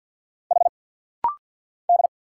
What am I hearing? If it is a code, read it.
Morse code, heard as a keyed tone, about 50 wpm, 700 Hz. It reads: H D
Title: Single Letters - Rapid-Fire 50wpm